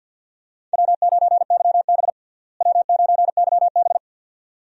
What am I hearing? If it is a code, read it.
Morse code, heard as a keyed tone, about 49 wpm, 700 Hz. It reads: W9YB W9YB
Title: USA Call Signs - 4 Characters - Repeated 2x - 50wpm